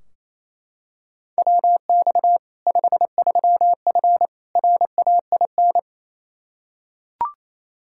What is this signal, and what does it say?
Morse code, heard as a keyed tone, about 28 wpm, 700 Hz. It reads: WX 53F RAIN
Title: Practice QSO - Exchange 2 - Info - 28wpm